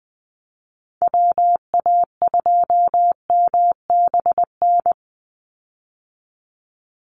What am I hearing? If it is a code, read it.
Morse code, heard as a keyed tone, about 20 wpm, 700 Hz. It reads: WA2MBN
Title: USA Call Signs 20wpm